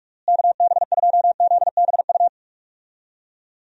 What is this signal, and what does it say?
Morse code, heard as a keyed tone, about 42 wpm, 700 Hz. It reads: KB2ZBU
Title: USA Call Signs - 6 Characters - Repeated 3x - 45wpm